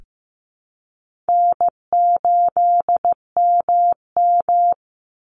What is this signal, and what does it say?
Morse code, heard as a keyed tone, about 15 wpm, 700 Hz. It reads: N8MM
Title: USA Call Signs 15wpm